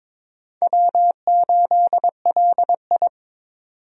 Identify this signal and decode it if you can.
Morse code, heard as a keyed tone, about 22 wpm, 700 Hz. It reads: W8LI
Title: USA Call Signs - 4 Characters - 22wpm